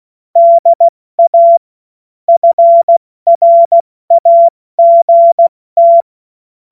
Morse code, heard as a keyed tone, about 16 wpm, 680 Hz. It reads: DA FRAGT
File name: Morsekurs / Witze / Witz 15. Juni 2021, 70 F80